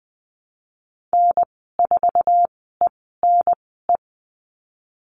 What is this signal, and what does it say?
Morse code, heard as a keyed tone, about 20 wpm, spaced as for 14 wpm, 700 Hz. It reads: N4ENE
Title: 048 Lesson 9g - Number 4 - Call Signs 20wpm